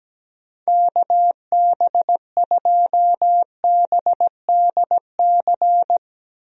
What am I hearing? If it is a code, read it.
Morse code, heard as a keyed tone, about 17 wpm, 700 Hz. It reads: KB2BDC
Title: USA Call Signs - 6 Characters - Repeated 2x - 17wpm